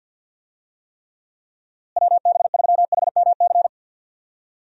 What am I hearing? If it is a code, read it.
Morse code, heard as a keyed tone, about 48 wpm, 700 Hz. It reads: WB3HMC
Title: USA Call Signs - 6 Characters - Repeated 2x - 50wpm